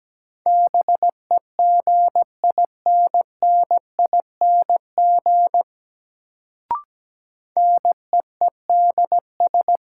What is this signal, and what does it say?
Morse code, heard as a keyed tone, about 17 wpm, 700 Hz. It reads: BEGINNING NEEDS
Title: Sets of 3 Words - New Word Drill - 17wpm